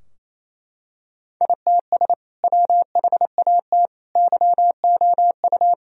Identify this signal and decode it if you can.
Morse code, heard as a keyed tone, about 27 wpm, 700 Hz. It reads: ITS WHAT YOU
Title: Sentences from Top 1000 Words - Review Part 3 - 28wpm